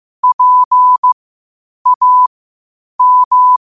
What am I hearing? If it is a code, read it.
Morse code, heard as a keyed tone, about 15 wpm, spaced as for 8 wpm, 1000 Hz. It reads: PAM
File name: qotdfiles / 8wpm